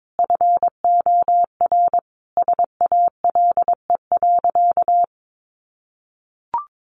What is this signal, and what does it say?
Morse code, heard as a keyed tone, about 22 wpm, 700 Hz. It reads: FOR SALE.
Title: Sentences from Top 1400 Words 22wpm